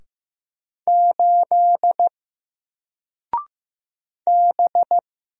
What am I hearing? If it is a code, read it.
Morse code, heard as a keyed tone, about 15 wpm, 700 Hz. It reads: 8 B
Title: Single Character (Letter:Number:Punctuation:Prosign) Alt 15wpm